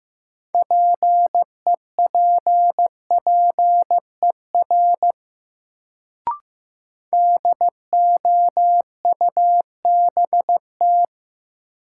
Morse code, heard as a keyed tone, about 15 wpm, 700 Hz. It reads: PEPPER DOUBT
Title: Sets of 5 Words - New Word Drill - 15wpm